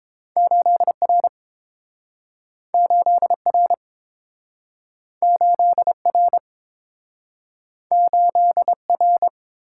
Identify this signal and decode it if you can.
Morse code, heard as a keyed tone, about 30 wpm, 700 Hz. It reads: 8R 8R 8R 8R